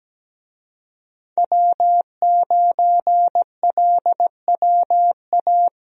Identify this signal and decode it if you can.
Morse code, heard as a keyed tone, about 17 wpm, 700 Hz. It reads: W9LWA
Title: USA Call Signs - 5 Characters - 17wpm